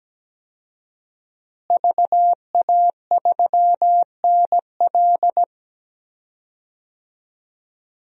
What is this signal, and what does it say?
Morse code, heard as a keyed tone, about 17 wpm, 700 Hz. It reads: VA3NL
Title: DXCC-100 Call Signs - 5 Characters - Repeated 3x - 17wpm